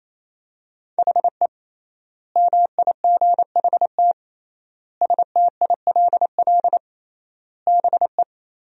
Morse code, heard as a keyed tone, about 28 wpm, 700 Hz. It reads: HE MIGHT STILL BE
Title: 3x Word-Spacing - Sentences from Top 300 Words - 28wpm